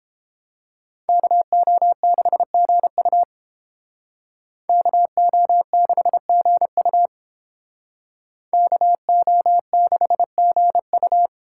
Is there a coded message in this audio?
Yes: Morse code from 1.1 to 11.3 s, heard as a keyed tone, about 29 wpm, 700 Hz. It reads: KO6GU KO6GU KO6GU